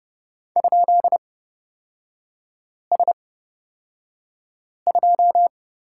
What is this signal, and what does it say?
Morse code, heard as a keyed tone, about 30 wpm, spaced as for 5 wpm, 700 Hz. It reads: ?S2
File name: Lesson 37e - Letter x - Three Characters 30wpm